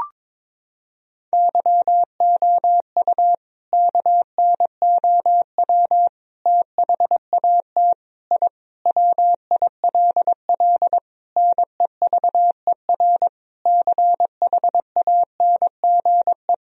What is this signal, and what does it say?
Morse code, heard as a keyed tone, about 21 wpm, 700 Hz. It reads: YOU KNOW THAT I WILL NEVER CHANGE
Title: Sentences from Top 1000 Words - Review Part 1 - 22wpm